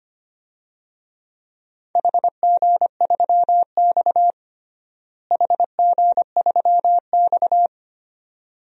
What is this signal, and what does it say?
Morse code, heard as a keyed tone, about 25 wpm, 700 Hz. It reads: HG3X HG3X
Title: DXCC-100 Call Signs - 4 Characters - Repeated 2x - 25wpm